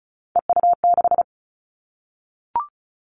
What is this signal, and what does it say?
Morse code, heard as a keyed tone, about 36 wpm, 700 Hz. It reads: EU6
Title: Callsign Prefixes for Top 100 DXCC - 35wpm